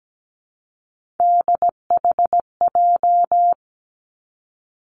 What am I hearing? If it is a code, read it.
Morse code, heard as a keyed tone, about 17 wpm, 700 Hz. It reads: DHJ
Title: Three Challenging Letter-Number Combinations 17wpm